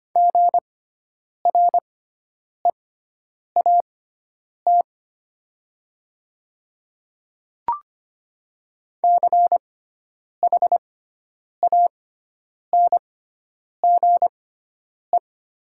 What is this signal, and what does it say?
Morse code, heard as a keyed tone, about 25 wpm, spaced as for 9 wpm, 700 Hz. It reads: GREAT CHANGE
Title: 6x Character Spacing - Top 200 Words - 25wpm